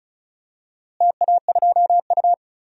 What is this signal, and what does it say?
Morse code, heard as a keyed tone, about 36 wpm, 700 Hz. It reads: TA2U
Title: DXCC-100 Call Signs - 4 Characters - Repeated 3x - 35wpm